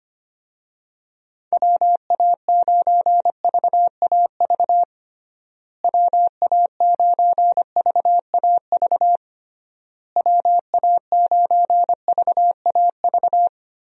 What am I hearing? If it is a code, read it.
Morse code, heard as a keyed tone, about 25 wpm, 700 Hz. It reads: WA9VAV WA9VAV WA9VAV
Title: USA Call Signs - 6 Characters - Repeated 3x - 25wpm